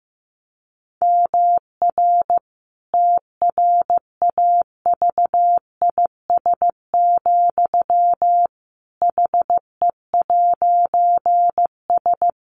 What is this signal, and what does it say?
Morse code, heard as a keyed tone, about 15 wpm, 700 Hz. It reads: MR TRAVIS, HE'S
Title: A Sound Of Thunder 15wpm